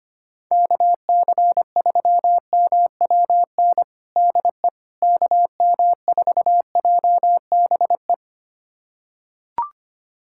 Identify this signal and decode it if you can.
Morse code, heard as a keyed tone, about 25 wpm, 700 Hz. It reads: KC3MWN DE KM4JBE
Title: USA Callsign DE Callsign - 6 Characters - 25wpm